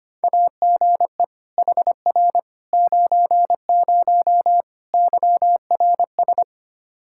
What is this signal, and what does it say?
Morse code, heard as a keyed tone, about 25 wpm, 700 Hz. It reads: AGE HR 90 YRS
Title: Practice QSO - Exchange 3 - More Info - 25wpm